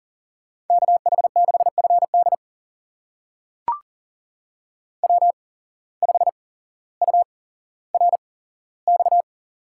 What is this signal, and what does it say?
Morse code, heard as a keyed tone, about 40 wpm, spaced as for 17 wpm, 700 Hz. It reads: KH6FD W5URX
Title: USA Call Signs - 5 Characters - 8x Character Spacing - 40wpm